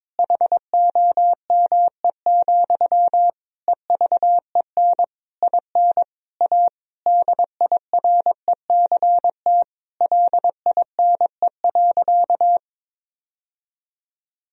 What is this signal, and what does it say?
Morse code, heard as a keyed tone, about 22 wpm, 700 Hz. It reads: HOME, EVEN IN A DIRECT LINE.
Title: Sentences from Top 1000 Words - Review Part 2 - 22wpm